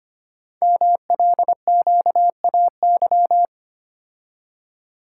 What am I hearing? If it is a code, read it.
Morse code, heard as a keyed tone, about 25 wpm, 700 Hz. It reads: MLQAY
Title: Code Groups - 5 Letters - 25wpm